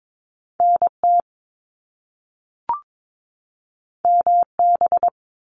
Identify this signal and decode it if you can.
Morse code, heard as a keyed tone, about 22 wpm, 700 Hz. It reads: NT MB
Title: Canadian Provinces and Territories 22wpm